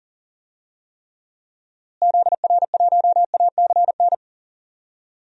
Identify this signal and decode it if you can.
Morse code, heard as a keyed tone, about 40 wpm, 700 Hz. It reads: ZR1ACN
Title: DXCC-100 Call Signs - 6 Characters - Repeated 3x - 40wpm